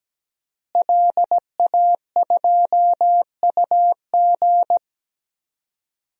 Morse code, heard as a keyed tone, about 17 wpm, 700 Hz. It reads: LA2UG
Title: DXCC-100 Call Signs - 5 Characters - Repeated 2x - 17wpm